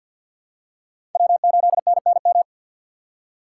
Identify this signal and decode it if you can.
Morse code, heard as a keyed tone, about 48 wpm, 700 Hz. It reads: W8NNM
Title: USA Call Signs - 6x Character Spacing - 50wpm